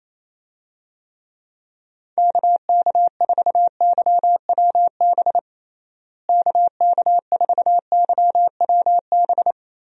Morse code, heard as a keyed tone, about 27 wpm, 700 Hz. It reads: KK4YWB KK4YWB
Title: USA Call Signs - 6 Characters - Repeated 3x - 28wpm